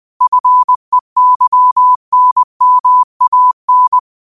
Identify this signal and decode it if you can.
Morse code, heard as a keyed tone, about 20 wpm, 1000 Hz. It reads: FEYNMAN